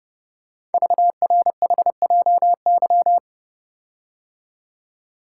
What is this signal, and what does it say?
Morse code, heard as a keyed tone, about 30 wpm, 700 Hz. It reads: VRHJY
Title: Code Groups - 5 Letter:Numbers - 30wpm